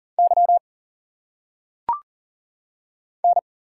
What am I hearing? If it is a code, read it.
Morse code, heard as a keyed tone, about 40 wpm, 700 Hz. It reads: Y N
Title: Single Letters - Phonetic Pronunciation 40wpm